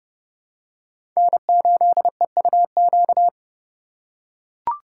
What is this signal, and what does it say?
Morse code, heard as a keyed tone, about 30 wpm, 700 Hz. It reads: N8EUQ
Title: USA Call Signs - 5 Characters - 8x Character Spacing - 30wpm